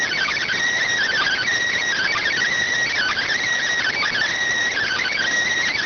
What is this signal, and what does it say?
Emergency Alert System (SAME) tones (header bursts), heard as harsh buzzing bursts. It is heard from 1.0 to 5.3 s.
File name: findings / MR 2 Pel Sag T2 SSFSE 0009.gen-38 rx